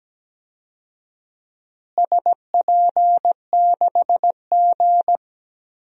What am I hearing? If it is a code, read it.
Morse code, heard as a keyed tone, about 17 wpm, 700 Hz. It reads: SP6G
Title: DXCC-100 Call Signs - 4 Characters - Repeated 2x - 17wpm